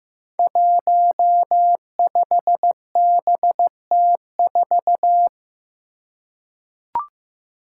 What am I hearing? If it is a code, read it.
Morse code, heard as a keyed tone, about 15 wpm, 700 Hz. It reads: 15BT4
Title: Five Challenging Letter-Number Combinations 15wpm